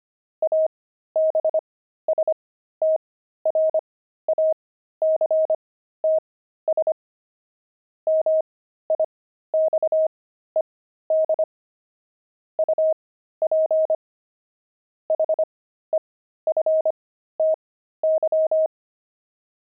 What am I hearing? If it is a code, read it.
Morse code, heard as a keyed tone, about 25 wpm, spaced as for 13 wpm, 630 Hz. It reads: ABSTRACTS MIXED UP HEFTY
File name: pangram-fs-13-cs-25-jocks